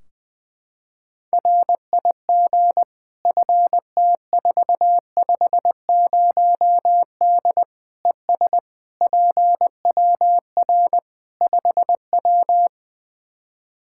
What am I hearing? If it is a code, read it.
Morse code, heard as a keyed tone, about 20 wpm, 700 Hz. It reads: RIG FT450D ES PWR 5W
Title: Practice QSO - Exchange 2 - Info - 20wpm